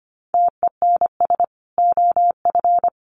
Morse code, heard as a keyed tone, about 25 wpm, 700 Hz. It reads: TENS OF
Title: Sets of 5 Words Encore 25wpm